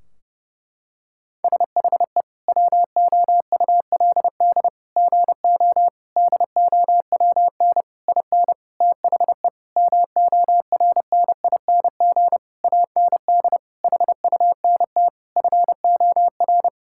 Morse code, heard as a keyed tone, about 30 wpm, 700 Hz. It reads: SHE WOULD GO DOWN IN THE MORNING AND HUNT FOR
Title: Sentences from Top 1600 Words 30wpm